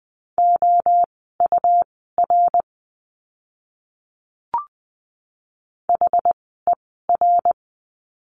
Morse code, heard as a keyed tone, about 20 wpm, spaced as for 14 wpm, 700 Hz. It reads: OUR HER
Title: 230 Lesson 37f - Letter x - Words 20wpm